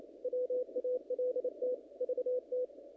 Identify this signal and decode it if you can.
Morse code, heard as a keyed tone, about 28 wpm, 500 Hz. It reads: WALT VT
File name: contests / naqp-cw-k5tr-op-k5pi- -1900